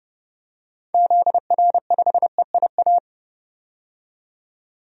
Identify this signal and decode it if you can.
Morse code, heard as a keyed tone, about 32 wpm, 700 Hz. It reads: ZR5EIA
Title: DXCC-100 Call Signs - 6 Characters - 30wpm